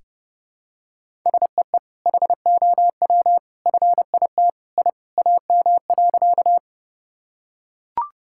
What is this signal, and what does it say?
Morse code, heard as a keyed tone, about 32 wpm, 700 Hz. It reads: SEE HOW FIT I AM.
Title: Sentences from Top 1000 Words - Review Part 5 - 30wpm